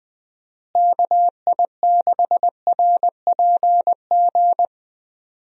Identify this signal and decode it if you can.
Morse code, heard as a keyed tone, about 20 wpm, 700 Hz. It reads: KI6RPG
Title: USA Call Signs - 6 Characters - Repeated 2x - 20wpm